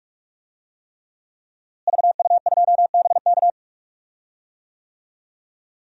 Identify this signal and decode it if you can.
Morse code, heard as a keyed tone, about 46 wpm, 700 Hz. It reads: VU2BK
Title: DXCC-100 Call Signs - 5 Characters - Repeated 3x - 45wpm